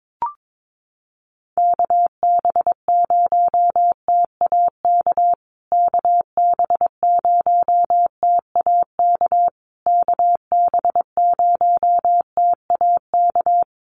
Morse code, heard as a keyed tone, about 22 wpm, 700 Hz. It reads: KB0TAK KB0TAK KB0TAK
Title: USA Call Signs Repeated 3x 22wpm